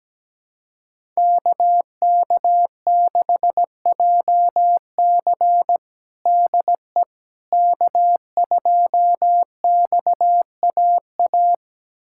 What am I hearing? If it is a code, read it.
Morse code, heard as a keyed tone, about 17 wpm, 700 Hz. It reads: KK6JC DE K2XAA
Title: USA Callsign DE Callsign - 5 Characters - 17wpm